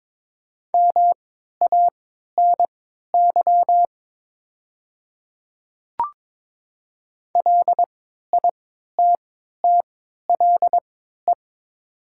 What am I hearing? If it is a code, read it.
Morse code, heard as a keyed tone, about 22 wpm, spaced as for 12 wpm, 700 Hz. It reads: MANY LITTLE
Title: 3x Character Spacing - Top 200 Words - 22wpm